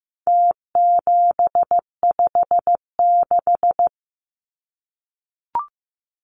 Morse code, heard as a keyed tone, about 15 wpm, 700 Hz. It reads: T756
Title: Numbers - Contest Style - 4-Digit 15wpm